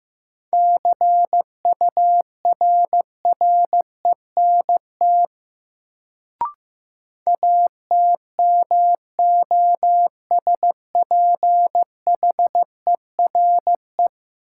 Morse code, heard as a keyed tone, about 15 wpm, 700 Hz. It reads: CURRENT ATMOSPHERE
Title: Sentences from Top 1600 Words - New Word Drill Alt - 15wpm